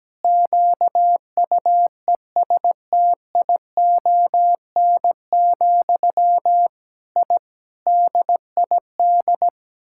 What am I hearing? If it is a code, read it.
Morse code, heard as a keyed tone, about 17 wpm, 700 Hz. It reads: QUESTION, I DID
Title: Sentences from Top 1000 Words - Review Part 6 - 17wpm